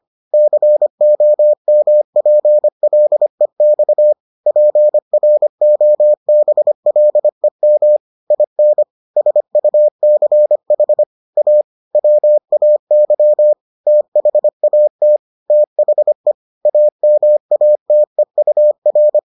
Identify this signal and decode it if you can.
Morse code, heard as a keyed tone, about 25 wpm, 600 Hz. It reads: COMPLEX PROBLEM IN SUCH A WAY THAT THE AMATEUR